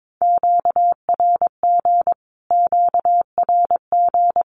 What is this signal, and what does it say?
Morse code, heard as a keyed tone, about 22 wpm, 700 Hz. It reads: QRG QRG
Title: Q-Codes Repeated 3x 22wpm